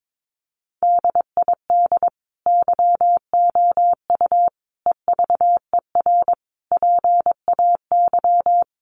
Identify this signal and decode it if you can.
Morse code, heard as a keyed tone, about 22 wpm, 700 Hz. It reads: DID YOU EVER PAY
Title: Sentences from Top 700 Words 22wpm